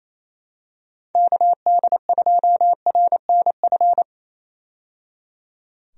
Morse code, heard as a keyed tone, about 28 wpm, 700 Hz. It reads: KD2RNF
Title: USA Call Signs - 6 Characters - 28wpm